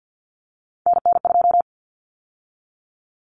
Morse code, heard as a keyed tone, about 48 wpm, 700 Hz. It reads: NN2
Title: Common Cut Numbers - 3-Digit 50wpm